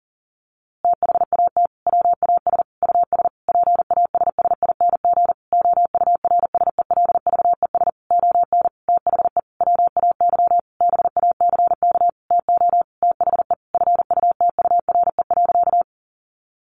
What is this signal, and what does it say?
Morse code, heard as a keyed tone, about 40 wpm, 700 Hz. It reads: THAT WAS US PASSING OURSELVES ON THE WAY BACK TO THE FUTURE.
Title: A Sound Of Thunder 40wpm